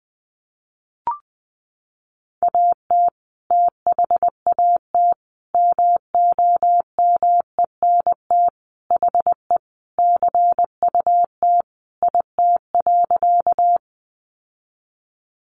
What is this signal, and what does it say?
Morse code, heard as a keyed tone, about 20 wpm, 700 Hz. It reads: AT THAT MOMENT HE CUT IT.
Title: Sentences from Top 1000 Words 20wpm